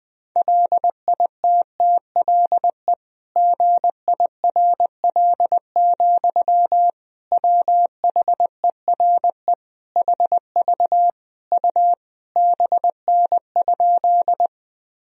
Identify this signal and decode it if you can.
Morse code, heard as a keyed tone, about 19 wpm, 700 Hz. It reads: LITTLE GIRL, WHERE HV U BN?
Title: Sentences from Top 400 Words and Abbreviations 20wpm